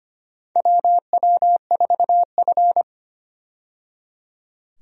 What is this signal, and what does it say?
Morse code, heard as a keyed tone, about 25 wpm, 700 Hz. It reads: WW4F
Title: USA Call Signs - 4 Characters - 25wpm